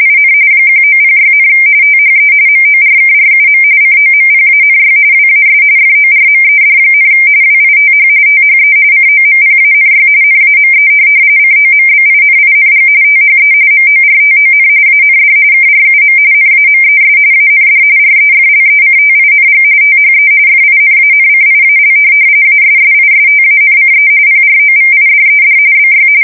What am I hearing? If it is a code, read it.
RTTY, about 45.45 baud. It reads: IN 1906, THE MORKRUM COMPANY WAS FORMED, WITH THE COMPANY NAME COMBINING THE MORTON AND KRUM NAMES AND REFLECTING THE FINANCIAL ASSISTANCE PROVIDED BY
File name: fsk170 / teletype-corp